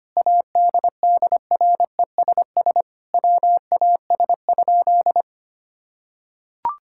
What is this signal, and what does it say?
Morse code, heard as a keyed tone, about 25 wpm, 700 Hz. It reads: ADDRESS WAS?
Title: Sentences from Top 1000 Words - Review Part 6 - 25wpm